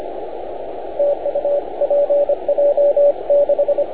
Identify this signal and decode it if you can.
Morse code, heard as a keyed tone, about 25 wpm, 570 Hz, with radio noise. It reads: TU PJ6